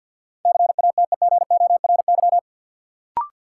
Morse code, heard as a keyed tone, about 53 wpm, 700 Hz. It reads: CATEGORY
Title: Sentences from Top 1000 Words - Review Part 7 - New Word Drill - 50wpm